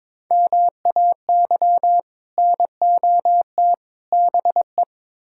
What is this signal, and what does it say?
Morse code, heard as a keyed tone, about 22 wpm, 700 Hz. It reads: MAY NOT BE
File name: practice / 22wpm / Build A Phrase - 5-Word Phrases - 22wpm